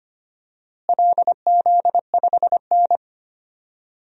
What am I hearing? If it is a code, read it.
Morse code, heard as a keyed tone, about 25 wpm, 700 Hz. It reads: LZ5N
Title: DXCC-100 Call Signs - 4 Characters - Repeated 2x - 25wpm